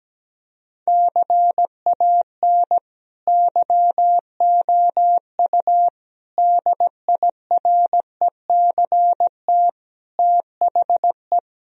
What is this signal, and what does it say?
Morse code, heard as a keyed tone, about 17 wpm, 700 Hz. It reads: CAN YOU DIRECT THE
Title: Sentences from Top 1000 Words - Review Part 2 - 17wpm